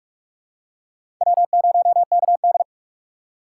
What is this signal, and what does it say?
Morse code, heard as a keyed tone, about 45 wpm, 700 Hz. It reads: W0KD
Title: USA Call Signs - 4 Characters - 3x Character Spacing - 45wpm